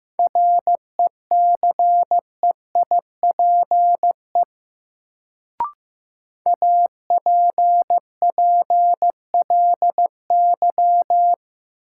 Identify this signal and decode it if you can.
Morse code, heard as a keyed tone, about 15 wpm, 700 Hz. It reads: RECEIPE APPLY
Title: Sentences from Top 1200 Words - New Word Drill Alt - 15wpm